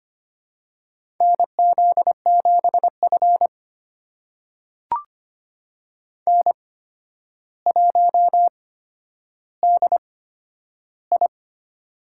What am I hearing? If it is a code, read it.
Morse code, heard as a keyed tone, about 25 wpm, spaced as for 11 wpm, 700 Hz. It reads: NZ7F N1DI
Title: USA Call Signs - 4 Characters - 8x Character Spacing - 25wpm